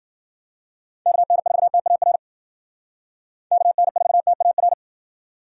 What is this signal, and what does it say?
Morse code, heard as a keyed tone, about 51 wpm, 700 Hz. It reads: KN4TAR KN4TAR